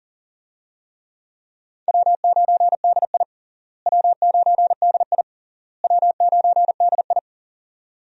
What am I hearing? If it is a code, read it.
Morse code, heard as a keyed tone, about 40 wpm, 700 Hz. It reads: W9DI W9DI W9DI
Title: USA Call Signs - 4 Characters - Repeated 3x - 40wpm